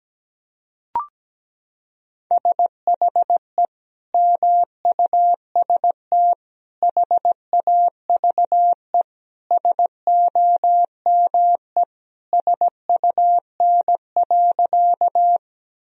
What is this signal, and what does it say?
Morse code, heard as a keyed tone, about 17 wpm, 700 Hz. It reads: SHE MUST HAVE SOME SUN.
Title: Sentences from Top 1000 Words - Review Part 1 - 17wpm